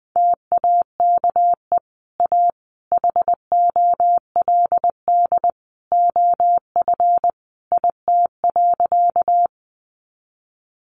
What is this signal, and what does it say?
Morse code, heard as a keyed tone, about 20 wpm, 700 Hz. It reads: TAKE A HOLD OF IT.
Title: Sentences from Top 400 Words 20wpm